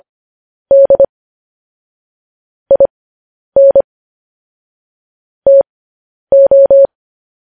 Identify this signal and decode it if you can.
Morse code, heard as a keyed tone, about 25 wpm, spaced as for 10 wpm, 550 Hz. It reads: D IN TO
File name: podcast / News headlines-1x-25x10